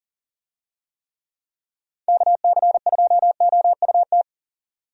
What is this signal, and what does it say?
Morse code, heard as a keyed tone, about 40 wpm, 700 Hz. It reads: KC2OUT
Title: USA Call Signs - 6 Characters - 40wpm